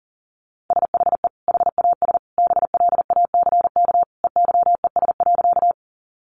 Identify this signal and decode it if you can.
Morse code, heard as a keyed tone, about 40 wpm, 700 Hz. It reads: SHE HAS BLACK EYES.
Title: Easy Phrases 40wpm